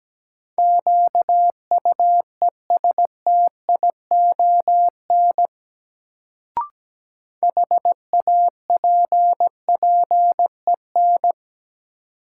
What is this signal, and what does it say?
Morse code, heard as a keyed tone, about 17 wpm, 700 Hz. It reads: QUESTION HAPPEN
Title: Sets of 5 Words Encore - New Word Drill - 17wpm